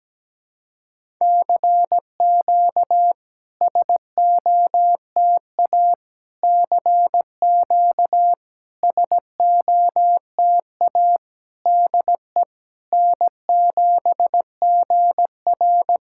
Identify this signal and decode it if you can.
Morse code, heard as a keyed tone, about 17 wpm, 690 Hz. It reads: CQ SOTA CQ SOTA DE N7GR
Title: SOTA QSO Practice 17wpm